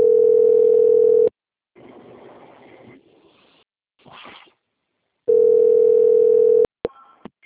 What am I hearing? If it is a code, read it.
Ringback tone (North America), 440 + 480 Hz.